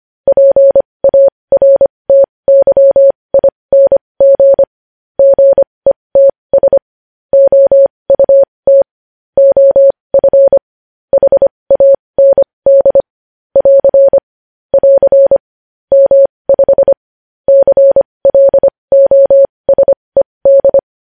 Morse code, heard as a keyed tone, about 25 wpm, spaced as for 20 wpm, 550 Hz. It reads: PARTYING GETS OUT OF HAND <AR> <AR> M5 CLOSED
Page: news-headlines-in-morse-code-at-20-wpm